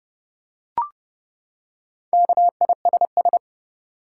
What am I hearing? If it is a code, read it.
Morse code, heard as a keyed tone, about 30 wpm, 700 Hz. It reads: KISS
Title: Four Letter Words - 30wpm